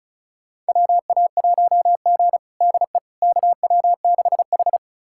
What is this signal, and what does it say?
Morse code, heard as a keyed tone, about 36 wpm, 700 Hz. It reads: WA1G DE KW6H
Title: USA Callsign DE Callsign - 4 Characters - 35wpm